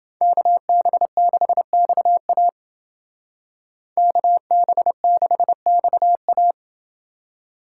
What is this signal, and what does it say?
Morse code, heard as a keyed tone, about 28 wpm, 700 Hz. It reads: KB6XA KB6XA